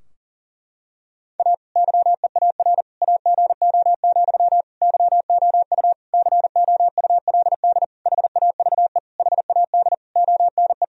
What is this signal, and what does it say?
Morse code, heard as a keyed tone, about 37 wpm, 700 Hz. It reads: A YEAR AGO, YOU COULD HAVE HAD ONE
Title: Sentences from Top 1000 Words - Review Part 3 - 40wpm